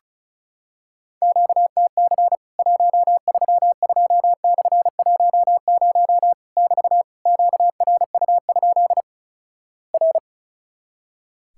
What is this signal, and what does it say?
Morse code, heard as a keyed tone, about 36 wpm, 700 Hz. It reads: QTC 132/10 <BT> QRU? R
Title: Worked All Europe DX Contest - QTC Exchange - 35wpm